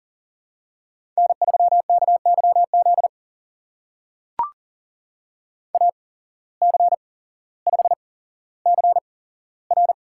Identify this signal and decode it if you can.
Morse code, heard as a keyed tone, about 39 wpm, spaced as for 17 wpm, 700 Hz. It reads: N3KYZ AC5CR
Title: USA Call Signs - 5 Characters - 8x Character Spacing - 40wpm